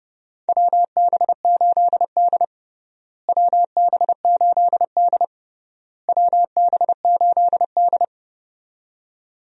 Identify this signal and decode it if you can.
Morse code, heard as a keyed tone, about 30 wpm, 700 Hz. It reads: WB8D WB8D WB8D
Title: USA Call Signs - 4 Characters - Repeated 3x - 30wpm